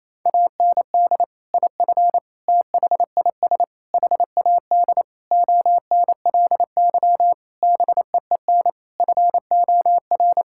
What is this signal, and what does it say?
Morse code, heard as a keyed tone, about 28 wpm, 700 Hz. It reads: AND IF THIS HAD ONLY BEEN FOR
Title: Sentences from Top 1600 Words 28wpm